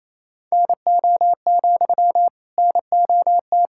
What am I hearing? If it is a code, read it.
Morse code, heard as a keyed tone, about 28 wpm, 690 Hz. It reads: NO, NOT
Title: Sentences from Top 1000 Words - Review Part 6 - 28wpm